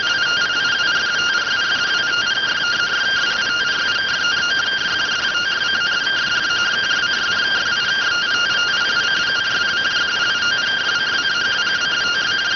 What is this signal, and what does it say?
Two-tone teleprinter signal, about 170 Hz shift.
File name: findings / CT 205 ax mips, iDose 3 0006.gen-28 rx